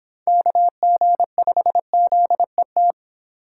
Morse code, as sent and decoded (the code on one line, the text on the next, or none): -.- --. ..... --.. -
KG5ZET